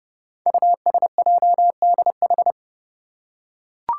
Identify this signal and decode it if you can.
Morse code, heard as a keyed tone, about 30 wpm, 700 Hz. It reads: USJDH
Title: Code Groups - 5 Letters - 30wpm